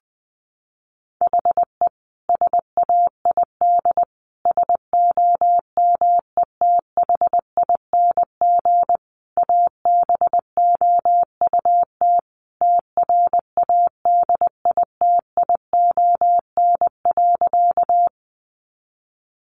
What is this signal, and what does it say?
Morse code, heard as a keyed tone, about 20 wpm, 700 Hz. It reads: HE SAID SOMETHING ABOUT TRADITION.
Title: Sentences from Top 1500 Words 20wpm